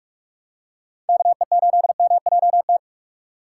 Morse code, heard as a keyed tone, about 46 wpm, 700 Hz. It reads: KE8MJT
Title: USA Call Signs - 6 Characters - 4x Character Spacing - 45wpm